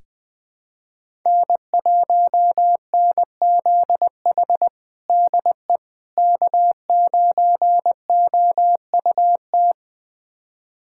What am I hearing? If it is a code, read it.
Morse code, heard as a keyed tone, about 20 wpm, 700 Hz. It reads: N1NZH DE K9OUT
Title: USA Callsign DE Callsign - 5 Characters - 20wpm